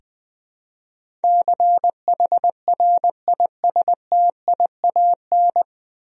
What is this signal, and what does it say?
Morse code, heard as a keyed tone, about 20 wpm, 700 Hz. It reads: CHRISTIAN
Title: USA Common Names - 20wpm